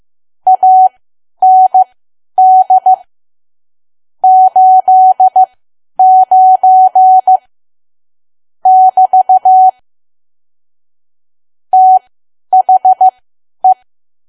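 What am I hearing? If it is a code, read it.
Morse code, heard as a keyed tone, about 15 wpm, spaced as for 10 wpm, 750 Hz. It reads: AND 89 <BT> THE
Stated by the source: Morse learning practice at 10 wpm.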